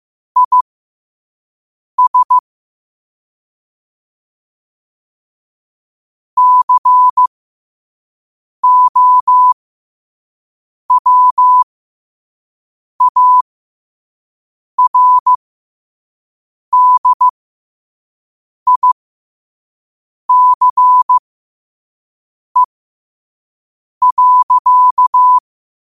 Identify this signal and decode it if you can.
Morse code, heard as a keyed tone, about 15 wpm, spaced as for 5 wpm, 1000 Hz. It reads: IS COWARDICE.
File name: qotdfiles / 5wpm